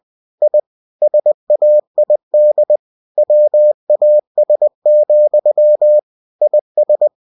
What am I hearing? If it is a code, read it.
Morse code, heard as a keyed tone, about 20 wpm, 600 Hz. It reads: I SAID WAS, IS
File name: cw-books / winnie-the-pooh / winnie-20wpm-page-0013